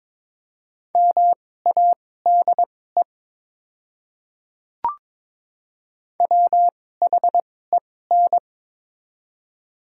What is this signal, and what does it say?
Morse code, heard as a keyed tone, about 22 wpm, spaced as for 16 wpm, 700 Hz. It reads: MADE WHEN
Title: 2x Character Spacing - Top 200 Words - 22wpm